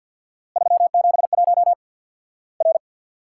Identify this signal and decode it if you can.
Morse code, heard as a keyed tone, about 50 wpm, 700 Hz. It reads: 371 R
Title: Worked All Europe DX Contest - QTC Exchange with Repeats - 50wpm